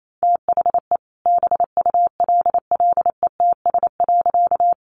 Morse code, heard as a keyed tone, about 28 wpm, 700 Hz. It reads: THE BULLETS.
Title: A Sound Of Thunder 28wpm